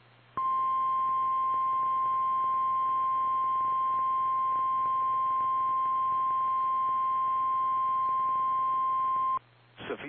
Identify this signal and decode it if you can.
Emergency Alert System (SAME) tones (the 1050 Hz attention tone). It is heard from 0.3 to 9.4 s.
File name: eas / KKPRKXPN / KKPRKXPN